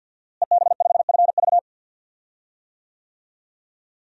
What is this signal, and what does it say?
Morse code, heard as a keyed tone, about 52 wpm, 700 Hz. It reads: EB5FV
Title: DXCC-100 Call Signs - 5 Characters - Repeated 3x - 50wpm